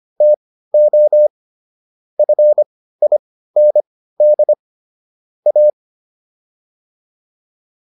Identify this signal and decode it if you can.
Morse code, heard as a keyed tone, about 25 wpm, spaced as for 15 wpm, 600 Hz. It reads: TO FIND A